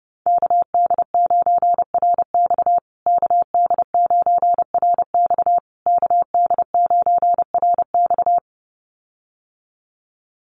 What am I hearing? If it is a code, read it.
Morse code, heard as a keyed tone, about 30 wpm, 700 Hz. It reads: KD9RX KD9RX KD9RX
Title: USA Call Signs Repeated 3x 30wpm